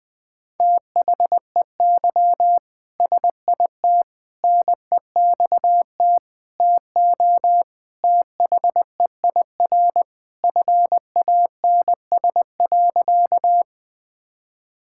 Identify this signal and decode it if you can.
Morse code, heard as a keyed tone, about 21 wpm, 700 Hz. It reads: THEY SIT NEXT TO THEIR FANS.
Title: All-Age Appropriate Jokes - 20wpm